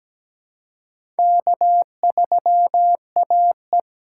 Morse code, heard as a keyed tone, about 17 wpm, 700 Hz. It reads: K3AE
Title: USA Call Signs - 4 Characters - 2x Character Spacing - 17wpm